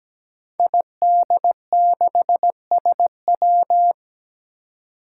Morse code, heard as a keyed tone, about 17 wpm, 700 Hz. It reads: ID6SW
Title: Five Challenging Letter-Number Combinations 17wpm